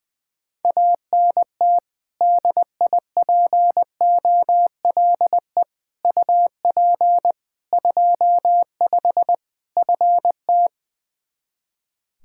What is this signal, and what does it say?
Morse code, heard as a keyed tone, about 20 wpm, 700 Hz. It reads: ANT DIPOLE UP 25 FT
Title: Practice QSO - Exchange 2 - Info - 20wpm